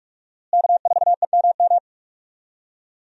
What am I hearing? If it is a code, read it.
Morse code, heard as a keyed tone, about 46 wpm, 700 Hz. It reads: K4EMM
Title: USA Call Signs - 5 Characters - Repeated 2x - 45wpm